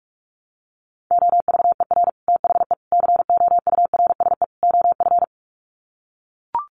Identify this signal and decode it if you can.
Morse code, heard as a keyed tone, about 44 wpm, 700 Hz. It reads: OVER THE COURSE OF
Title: Sets of 4 Words 45wpm